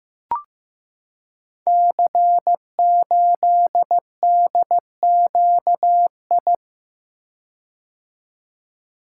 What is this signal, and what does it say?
Morse code, heard as a keyed tone, about 15 wpm, 700 Hz. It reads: C8DQI
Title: Code Groups - 5 Letter:Numbers - 15wpm